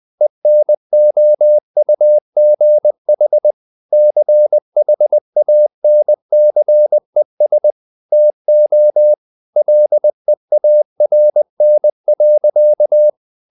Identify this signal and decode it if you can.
Morse code, heard as a keyed tone, about 20 wpm, 600 Hz. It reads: ENOUGH CHANCES TO LEARN.
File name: the old man and the sea part 2 20wpm